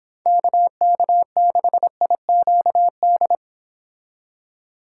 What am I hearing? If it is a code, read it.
Morse code, heard as a keyed tone, about 26 wpm, 700 Hz. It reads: KK6IQD